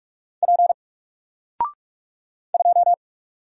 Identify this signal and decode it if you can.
Morse code, heard as a keyed tone, about 43 wpm, 700 Hz. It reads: P 2
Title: Single Character (Letter:Number:Punctuation:Prosign) Alt 45wpm